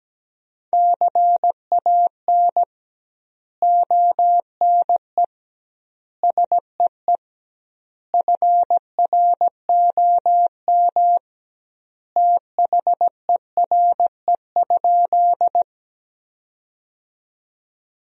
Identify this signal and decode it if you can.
Morse code, heard as a keyed tone, about 17 wpm, 700 Hz. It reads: CAN ONE SEE FROM THERE?
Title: 2x Word-Spacing - Sentences from Top 100 Words - 17wpm